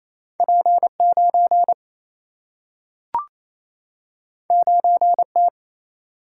Morse code, heard as a keyed tone, about 28 wpm, 700 Hz. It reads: P9 9T
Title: Single-Letter Single-Number 28wpm